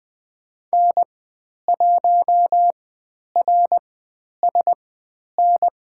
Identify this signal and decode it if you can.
Morse code, heard as a keyed tone, about 20 wpm, spaced as for 10 wpm, 700 Hz. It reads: N1RSN
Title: 109 Lesson 18g - Letter m - Call Signs 20wpm